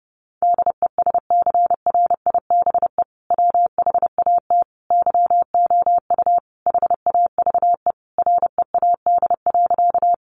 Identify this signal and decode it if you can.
Morse code, heard as a keyed tone, about 30 wpm, 700 Hz. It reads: DESCRIBE WHAT YOU HAVE READ.
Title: Sentences from Top 1000 Words 30wpm